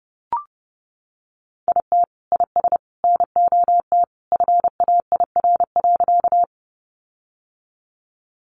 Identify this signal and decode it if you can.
Morse code, heard as a keyed tone, about 30 wpm, 700 Hz. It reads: IT IS NOT FAIR.
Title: Sentences from Top 1300 Words 30wpm